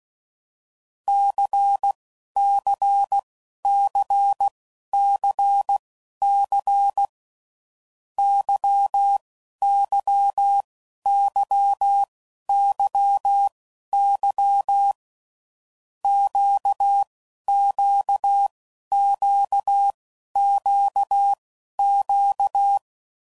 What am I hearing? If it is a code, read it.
Morse code, heard as a keyed tone, about 16 wpm, spaced as for 11 wpm, 800 Hz. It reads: CCCCC YYYYY QQQQQ
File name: Groups / Basic Letter Groups / LFCYQ